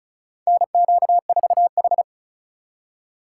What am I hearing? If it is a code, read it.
Morse code, heard as a keyed tone, about 36 wpm, 700 Hz. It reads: NQ4H
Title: USA Call Signs - 8x Character Spacing - 35wpm